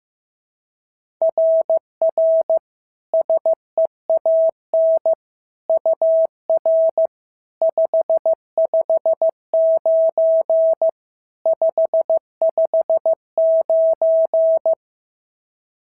Morse code, heard as a keyed tone, about 14 wpm, 650 Hz. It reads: RR SEAN UR 559 559
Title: SOTA QSO Practice 15wpm